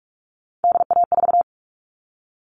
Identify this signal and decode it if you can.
Morse code, heard as a keyed tone, about 44 wpm, 700 Hz. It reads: DA4
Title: All Cut Numbers - 3-Digit 45wpm